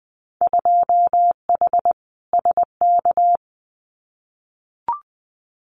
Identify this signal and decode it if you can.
Morse code, heard as a keyed tone, about 20 wpm, 700 Hz. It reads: 2H SK
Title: Winter Field Day Exchanges Encore 20wpm